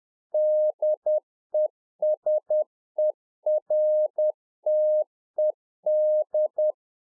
Morse code, heard as a keyed tone, about 10 wpm, 610 Hz. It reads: DESERTED